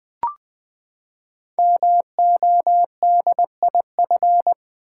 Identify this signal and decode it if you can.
Morse code, heard as a keyed tone, about 20 wpm, 700 Hz. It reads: MODIF
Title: Build A Word - 5 Letters At A Time - 12-Letter Words - 20wpm